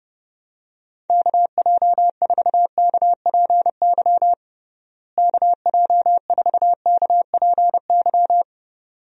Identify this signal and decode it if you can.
Morse code, heard as a keyed tone, about 30 wpm, 700 Hz. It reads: KJ4KPY KJ4KPY
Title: USA Call Signs - 6 Characters - Repeated 2x - 30wpm